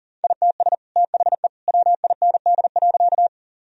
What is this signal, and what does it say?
Morse code, heard as a keyed tone, about 40 wpm, 700 Hz. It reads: ITS THE WIND.
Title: Sentences from Top 1000 Words - Review Part 2 - 40wpm